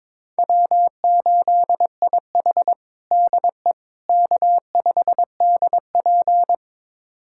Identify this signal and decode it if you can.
Morse code, heard as a keyed tone, about 21 wpm, 700 Hz. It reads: W8IH DE K5DP
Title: USA Callsign DE Callsign - 4 Characters - 22wpm